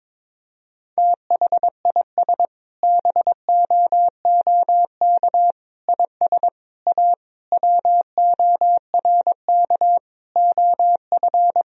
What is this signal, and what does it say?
Morse code, heard as a keyed tone, about 22 wpm, 700 Hz. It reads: THIS BOOK IS A WORK OF
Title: Sentences from Top 1000 Words - Review Part 3 - 22wpm